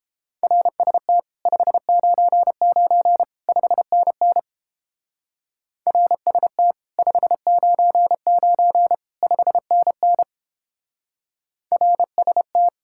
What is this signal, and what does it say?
Morse code, heard as a keyed tone, about 31 wpm, 700 Hz. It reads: RST 599 5NN RST 599 5NN RST